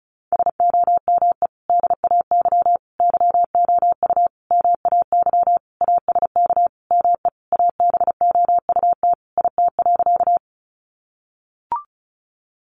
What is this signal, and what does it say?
Morse code, heard as a keyed tone, about 34 wpm, 700 Hz. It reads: SOME DAY YOU MAY ASK ME ABOUT IT.
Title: Sentences from Top 200 Words 35wpm